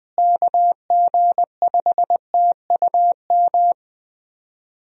Morse code, heard as a keyed tone, about 20 wpm, 700 Hz. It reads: KG5TUM
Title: USA Call Signs - 6 Characters - Repeated 3x - 20wpm